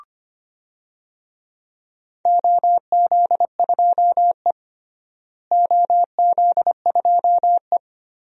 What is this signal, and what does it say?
Morse code, heard as a keyed tone, about 25 wpm, 700 Hz. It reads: OZ2E OZ2E
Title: DXCC-100 Call Signs - 4 Characters - Repeated 2x - 25wpm